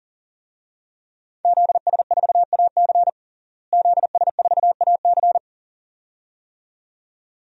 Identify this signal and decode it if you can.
Morse code, heard as a keyed tone, about 40 wpm, 700 Hz. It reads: ZS4AC ZS4AC
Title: DXCC-100 Call Signs - 5 Characters - Repeated 2x - 40wpm